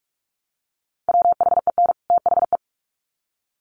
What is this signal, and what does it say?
Morse code, heard as a keyed tone, about 44 wpm, 700 Hz. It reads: WHEN THE
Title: Sets of 2 Words 45wpm